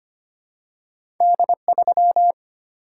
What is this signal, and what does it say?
Morse code, heard as a keyed tone, about 25 wpm, 700 Hz. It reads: D3
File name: Two Letter-Number Combinations - 25wpm